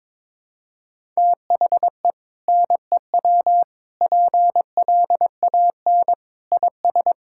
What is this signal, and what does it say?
Morse code, heard as a keyed tone, about 22 wpm, 700 Hz. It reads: THE NEW PLAN IS
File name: practice / 22wpm / Sentences with 2-Word Phrases - 22wpm